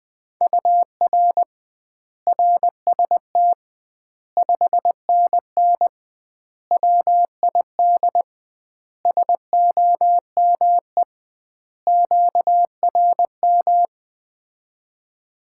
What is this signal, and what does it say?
Morse code, heard as a keyed tone, about 20 wpm, 700 Hz. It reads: UR RST 5NN WID SOME QRM
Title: Common QSO Phrases - 2x Word-Spacing - 20wpm